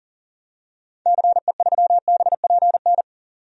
Morse code, heard as a keyed tone, about 40 wpm, 700 Hz. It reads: CE3BPN
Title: DXCC-100 Call Signs - 6 Characters - 40wpm